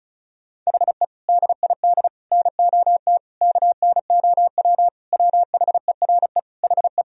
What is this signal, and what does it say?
Morse code, heard as a keyed tone, about 36 wpm, 700 Hz. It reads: HE DID NOT KNOW WHERE HE
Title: Sentences from Top 1000 Words - Review Part 2 - 35wpm